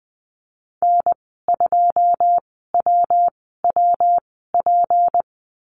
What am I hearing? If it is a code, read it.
Morse code, heard as a keyed tone, about 20 wpm, spaced as for 15 wpm, 700 Hz. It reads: N2WWP
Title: 167 Lesson 27g - Number 7 - Call Signs 20wpm